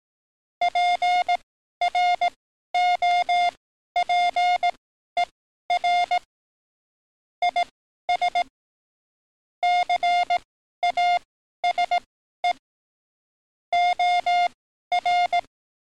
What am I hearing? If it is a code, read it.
Morse code, heard as a keyed tone, about 18 wpm, spaced as for 12 wpm, 710 Hz. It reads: PROPER IS CASE OR
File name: training / NAVPERS 11049to11060 / NAVPERS 11053 RA RB